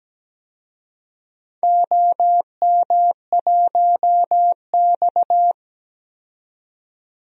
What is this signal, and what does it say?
Morse code, heard as a keyed tone, about 17 wpm, 700 Hz. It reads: OM1X
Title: DXCC-100 Call Signs - 4 Characters - 17wpm